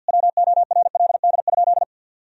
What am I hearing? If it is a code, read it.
Morse code, heard as a keyed tone, about 51 wpm, 700 Hz. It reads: WORLD?
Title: Sentences from Top 200 Words and Abbreviations 50wpm